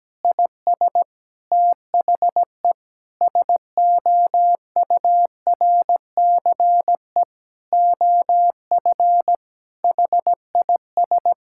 Morse code, heard as a keyed tone, about 17 wpm, 700 Hz. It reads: IS THE SOURCE OF HIS
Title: Sentences from Top 1000 Words - Review Part 5 - 17wpm